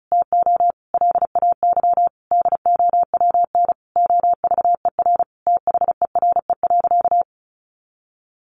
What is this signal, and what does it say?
Morse code, heard as a keyed tone, about 34 wpm, 700 Hz. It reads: TO LAY DOWN OVER THERE.
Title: Sentences from Top 800 Words 35wpm